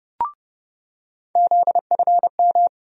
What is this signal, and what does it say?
Morse code, heard as a keyed tone, about 30 wpm, 700 Hz. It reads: ZFM
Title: USA Call Sign Postfixes - 3 Letters 30wpm